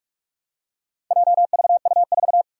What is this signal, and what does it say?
Morse code, heard as a keyed tone, about 44 wpm, 700 Hz. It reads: JVU4
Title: Four Challenging Letter-Number Combinations 45wpm